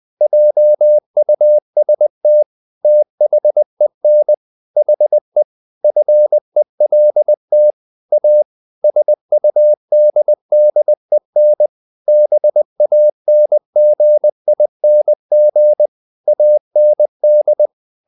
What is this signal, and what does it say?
Morse code, heard as a keyed tone, about 20 wpm, 600 Hz. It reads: JUST THEN HE FELT A SUDDEN BANGING AND
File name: the old man and the sea part 2 20wpm